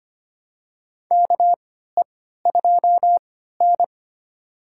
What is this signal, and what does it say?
Morse code, heard as a keyed tone, about 25 wpm, spaced as for 14 wpm, 700 Hz. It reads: KE2N
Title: USA Call Signs - 4 Characters - 3x Character Spacing - 25wpm